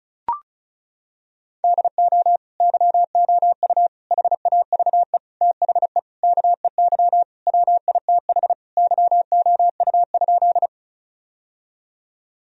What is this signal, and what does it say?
Morse code, heard as a keyed tone, about 36 wpm, 700 Hz. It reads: DO YOU HAVE THE KEY WITH YOU?
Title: Sentences from Top 1000 Words - Review Part 4 - 35wpm